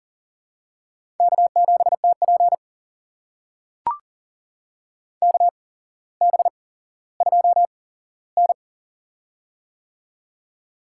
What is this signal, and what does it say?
Morse code, heard as a keyed tone, about 40 wpm, spaced as for 17 wpm, 700 Hz. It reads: K7TP KB2N
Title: USA Call Signs - 4 Characters - 8x Character Spacing - 40wpm